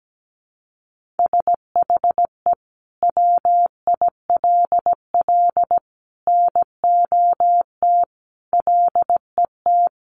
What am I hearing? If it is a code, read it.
Morse code, heard as a keyed tone, about 17 wpm, 700 Hz. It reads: SHE WILL NOT LET
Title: Sentences from Top 1500 Words 17wpm